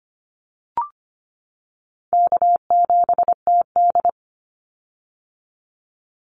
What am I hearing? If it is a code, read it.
Morse code, heard as a keyed tone, about 25 wpm, 700 Hz. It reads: K7TD
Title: Call Signs - 25wpm